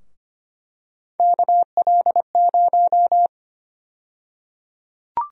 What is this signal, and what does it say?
Morse code, heard as a keyed tone, about 25 wpm, 700 Hz. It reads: KL0
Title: USA Call Sign Prefixes - Repeated 2x - 25wpm